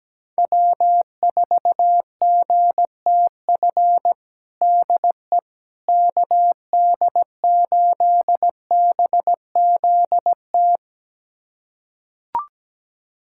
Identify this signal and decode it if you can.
Morse code, heard as a keyed tone, about 17 wpm, 700 Hz. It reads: W4GTF DE KD8BZT
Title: USA Callsign DE Callsign 17wpm